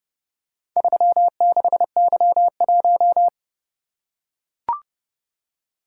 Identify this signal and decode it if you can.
Morse code, heard as a keyed tone, about 27 wpm, 700 Hz. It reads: 36Y1